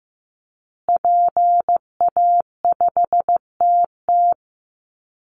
Morse code, heard as a keyed tone, about 15 wpm, 700 Hz. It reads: PA5TT
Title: Call Signs Repeated 2x - 15wpm